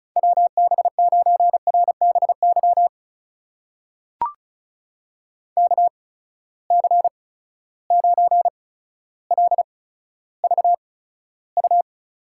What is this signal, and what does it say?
Morse code, heard as a keyed tone, about 36 wpm, spaced as for 15 wpm, 700 Hz. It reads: WB9RBY KC9LVU
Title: USA Call Signs - 6 Characters - 8x Character Spacing - 35wpm